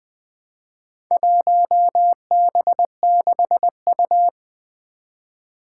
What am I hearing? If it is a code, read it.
Morse code, heard as a keyed tone, about 20 wpm, 700 Hz. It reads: 1B6U
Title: Four Challenging Letter-Number Combinations 20wpm